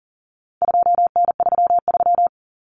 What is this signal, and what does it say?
Morse code, heard as a keyed tone, about 40 wpm, 700 Hz. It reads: 2N33
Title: Common Cut Numbers - 4-Digit 40wpm